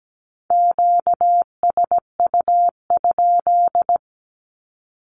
Morse code, heard as a keyed tone, about 17 wpm, 700 Hz. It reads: QSU?
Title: Q-Codes 17wpm